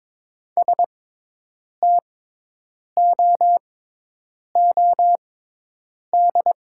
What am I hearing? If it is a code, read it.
Morse code, heard as a keyed tone, about 21 wpm, spaced as for 8 wpm, 700 Hz. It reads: STOOD
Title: 6x Character Spacing - Top 500 Words - 22wpm